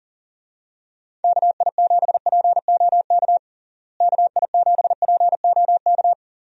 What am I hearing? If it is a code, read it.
Morse code, heard as a keyed tone, about 40 wpm, 700 Hz. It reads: KI7POK KI7POK
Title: USA Call Signs - 6 Characters - Repeated 3x - 40wpm